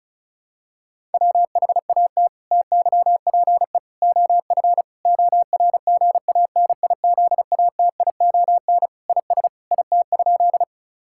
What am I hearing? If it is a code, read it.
Morse code, heard as a keyed tone, about 36 wpm, 700 Hz. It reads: WHAT TYPE OF ORGANIZATION IS IT?
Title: Sentences from Top 1000 Words - Review Part 5 - 35wpm